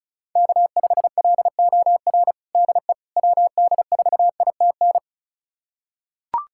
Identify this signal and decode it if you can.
Morse code, heard as a keyed tone, about 34 wpm, 700 Hz. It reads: K5LOR DE WD4ITN
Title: USA Callsign DE Callsign 35wpm